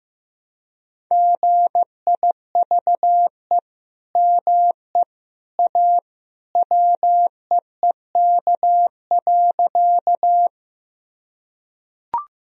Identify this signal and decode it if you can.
Morse code, heard as a keyed tone, about 16 wpm, 700 Hz. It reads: GIVE ME A WEEK.
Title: Sentences from Top 1000 Words - Review Part 2 - 15wpm